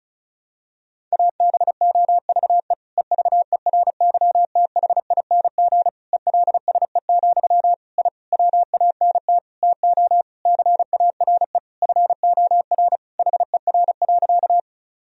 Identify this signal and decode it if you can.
Morse code, heard as a keyed tone, about 36 wpm, 700 Hz. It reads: ABOVE EVERYTHING ELSE, I WANT TO CARE FOR HER.
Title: Sentences from Top 1000 Words - Review Part 2 - 35wpm